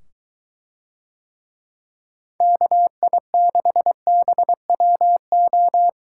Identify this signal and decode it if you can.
Morse code, heard as a keyed tone, about 23 wpm, 700 Hz. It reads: KI6BWO